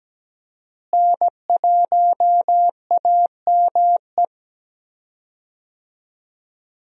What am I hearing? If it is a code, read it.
Morse code, heard as a keyed tone, about 17 wpm, 700 Hz. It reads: N1AME
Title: USA Call Signs - 5 Characters - Repeated 2x - 17wpm